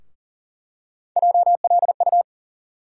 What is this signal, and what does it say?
Morse code, heard as a keyed tone, about 40 wpm, 700 Hz. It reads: JLU
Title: USA Call Sign Postfixes - 3 Letters 40wpm